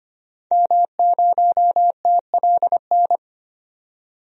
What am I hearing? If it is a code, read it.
Morse code, heard as a keyed tone, about 25 wpm, 700 Hz. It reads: M0TLN
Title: DXCC-100 Call Signs - 5 Characters - 25wpm